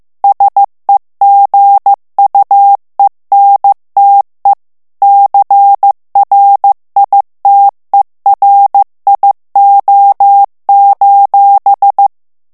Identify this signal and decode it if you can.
Morse code, heard as a keyed tone, about 15 wpm, 800 Hz. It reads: SEGUENTE CRITERIO:
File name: morse mp3 / ITALIEN texte clair 2 vitesse 14 mots minute